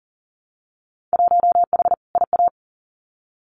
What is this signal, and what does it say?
Morse code, heard as a keyed tone, about 40 wpm, 700 Hz. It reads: 1H IA
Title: Winter Field Day Exchanges 40wpm